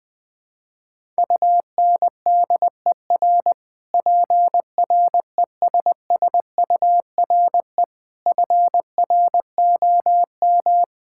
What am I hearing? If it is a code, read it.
Morse code, heard as a keyed tone, about 20 wpm, 700 Hz. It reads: UNDER PRESSURE FROM
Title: Sentences from Top 1000 Words - Review Part 6 - 20wpm